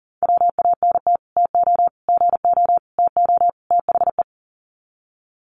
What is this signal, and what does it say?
Morse code, heard as a keyed tone, about 40 wpm, 700 Hz. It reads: WANT TO GO TO THE
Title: Sets of 5 Words Encore2 40wpm